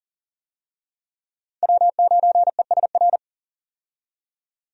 Morse code, heard as a keyed tone, about 40 wpm, 700 Hz. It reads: W9ESR
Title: USA Call Signs - 5 Characters - 40wpm